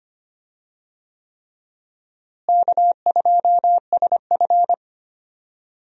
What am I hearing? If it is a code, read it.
Morse code, heard as a keyed tone, about 25 wpm, 700 Hz. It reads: K2SF
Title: USA Call Signs - 4 Characters - Repeated 3x - 25wpm